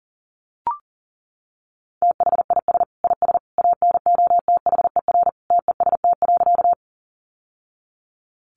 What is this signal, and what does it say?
Morse code, heard as a keyed tone, about 40 wpm, 700 Hz. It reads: THIS IS ANOTHER TEST.
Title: Sentences from Top 600 Words 40wpm